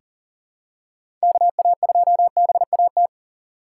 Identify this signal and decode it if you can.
Morse code, heard as a keyed tone, about 40 wpm, 700 Hz. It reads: KA2BAT
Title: USA Call Signs - 6 Characters - 2x Character Spacing - 40wpm